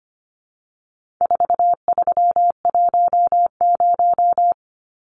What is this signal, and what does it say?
Morse code, heard as a keyed tone, about 25 wpm, 700 Hz. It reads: 4310
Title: Numbers - 4-Digit 25wpm